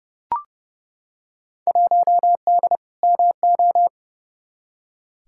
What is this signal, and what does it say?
Morse code, heard as a keyed tone, about 30 wpm, 700 Hz. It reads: 1D MO
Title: Field Day Exchanges - 30wpm